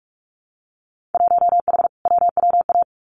Morse code, heard as a keyed tone, about 46 wpm, 700 Hz. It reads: 1H WWA
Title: Winter Field Day Exchanges 45wpm